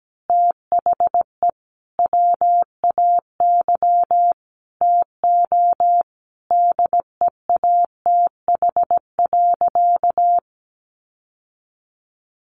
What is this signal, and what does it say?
Morse code, heard as a keyed tone, about 17 wpm, 700 Hz. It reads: THE WAY TO DEATH.
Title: Sentences from Top 1400 Words 17wpm